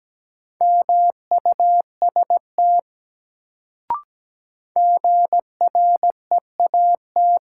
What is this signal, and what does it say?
Morse code, heard as a keyed tone, about 17 wpm, 700 Hz. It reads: MUST GREAT
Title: Sentences from Top 200 Words - New Word Drill - 17wpm